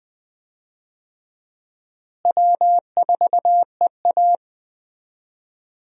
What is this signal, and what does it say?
Morse code, heard as a keyed tone, about 20 wpm, 700 Hz. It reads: W4EA
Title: USA Call Signs - 4 Characters - Repeated 3x - 20wpm